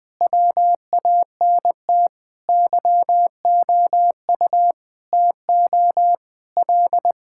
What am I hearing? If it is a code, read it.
Morse code, heard as a keyed tone, about 21 wpm, 700 Hz. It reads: WANT YOU TO L
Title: Sentences with 5-Word Phrases - Spoken Beginning - 20wpm